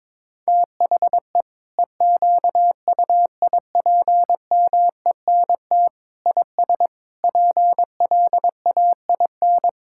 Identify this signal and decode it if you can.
Morse code, heard as a keyed tone, about 22 wpm, 700 Hz. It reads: THE EQUIPMENT IS PLAIN
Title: Sentences from Top 1000 Words - Review Part 5 - 22wpm